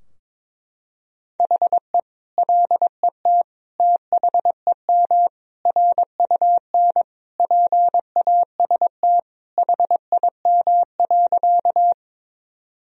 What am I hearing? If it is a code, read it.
Morse code, heard as a keyed tone, about 22 wpm, 700 Hz. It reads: HE LET THEM RUN PAST HIM.
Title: Sentences from Top 1000 Words - Review Part 3 - 22wpm